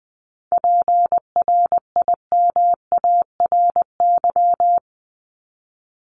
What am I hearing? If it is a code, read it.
Morse code, heard as a keyed tone, about 20 wpm, 700 Hz. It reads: PRIMARY
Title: LY Suffix - Top 100 Words with Base Words 20wpm